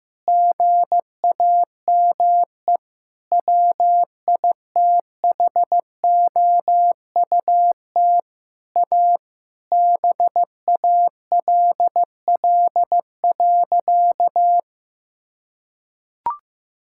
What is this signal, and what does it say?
Morse code, heard as a keyed tone, about 15 wpm, 700 Hz. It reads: GAME WITHOUT A BALL.
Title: Sentences from Top 1000 Words - Review Part 6 - 15wpm